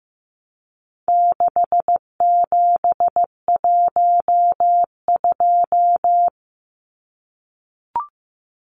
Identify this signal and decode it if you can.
Morse code, heard as a keyed tone, about 15 wpm, 700 Hz. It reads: 6712
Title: Numbers - 4-Digit 15wpm